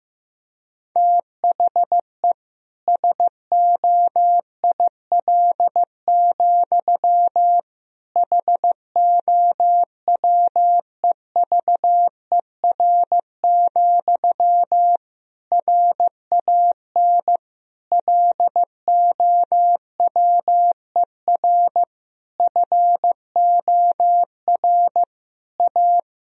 Morse code, heard as a keyed tone, about 15 wpm, 700 Hz. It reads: THE SOIL, HOWEVER, RAN LOWER FOR A
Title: Sentences from Top 1000 Words - Review Part 4 - 15wpm